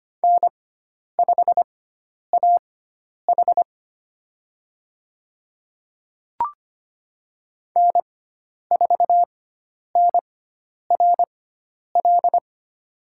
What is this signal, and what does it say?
Morse code, heard as a keyed tone, about 25 wpm, spaced as for 10 wpm, 700 Hz. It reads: N5AH N4NRL
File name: Lesson 16g - Letter c - Call Signs 25wpm